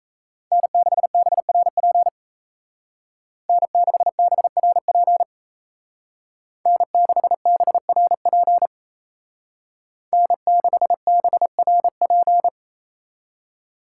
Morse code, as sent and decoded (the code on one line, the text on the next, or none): -. -.... -... .-. .--. / -. -.... -... .-. .--. / -. -.... -... .-. .--. / -. -.... -... .-. .--.
N6BRP N6BRP N6BRP N6BRP